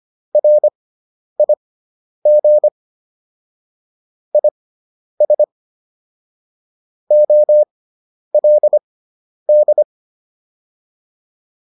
Morse code, heard as a keyed tone, about 25 wpm, spaced as for 10 wpm, 600 Hz. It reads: RIG IS OLD